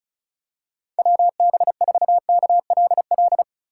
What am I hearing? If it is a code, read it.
Morse code, heard as a keyed tone, about 36 wpm, 700 Hz. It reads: WB4KLL
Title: USA Call Signs - 6x Character Spacing - 35wpm